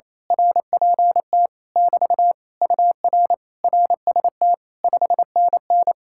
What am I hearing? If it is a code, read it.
Morse code, heard as a keyed tone, about 26 wpm, 700 Hz. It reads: RPT <BT> UR RST 5NN
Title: Speed-Racing Common QSO Phrases Encore - 28wpm